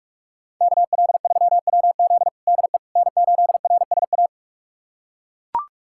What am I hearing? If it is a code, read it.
Morse code, heard as a keyed tone, about 41 wpm, 700 Hz. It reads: KL3WZ DE N8RSA
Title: USA Callsign DE Callsign - 5 Characters - 45wpm